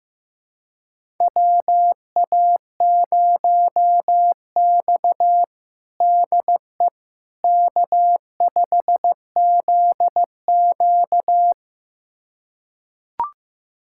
Morse code, heard as a keyed tone, about 15 wpm, 700 Hz. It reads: WA0X DE K5ZQ
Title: USA Callsign DE Callsign - 4 Characters - 15wpm